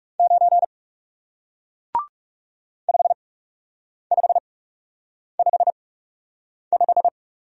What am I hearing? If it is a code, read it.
Morse code, heard as a keyed tone, about 40 wpm, spaced as for 8 wpm, 700 Hz. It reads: 9 5555